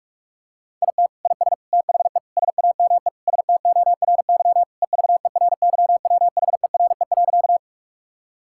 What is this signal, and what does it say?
Morse code, heard as a keyed tone, about 46 wpm, 700 Hz. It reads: IT IS THE SAME STORY EVERYWHERE.
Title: Sentences from Top 1600 Words 45wpm